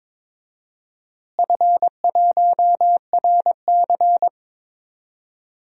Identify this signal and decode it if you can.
Morse code, heard as a keyed tone, about 22 wpm, 700 Hz. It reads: F1RC
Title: DXCC-100 Call Signs - 4 Characters - Repeated 3x - 22wpm